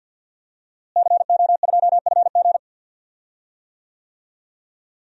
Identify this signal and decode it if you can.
Morse code, heard as a keyed tone, about 46 wpm, 700 Hz. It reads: CO2FG
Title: DXCC-100 Call Signs - 5 Characters - Repeated 2x - 50wpm